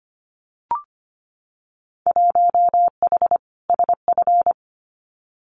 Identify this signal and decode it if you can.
Morse code, heard as a keyed tone, about 25 wpm, 700 Hz. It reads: 1H SF
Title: Winter Field Day Exchanges 25wpm